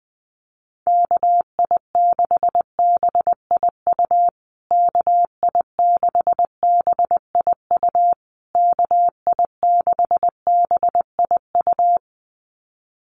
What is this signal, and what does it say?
Morse code, heard as a keyed tone, about 20 wpm, 700 Hz. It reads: KI6BIU KI6BIU KI6BIU
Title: USA Call Signs Repeated 3x 20wpm